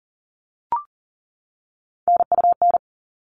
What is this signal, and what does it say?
Morse code, heard as a keyed tone, about 40 wpm, 700 Hz. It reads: NUN
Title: All Cut Numbers - 3-Digit 40wpm